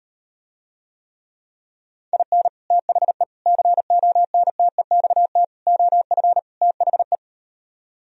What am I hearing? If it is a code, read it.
Morse code, heard as a keyed tone, about 39 wpm, 700 Hz. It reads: IN THE CONTEXT OF THE